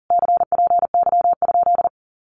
Morse code, heard as a keyed tone, about 37 wpm, 700 Hz. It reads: CPY?
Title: Common QSO Phrases Encore 40wpm